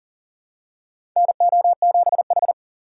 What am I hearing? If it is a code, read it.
Morse code, heard as a keyed tone, about 40 wpm, 700 Hz. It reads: NO7H
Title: USA Call Signs - 6x Character Spacing - 40wpm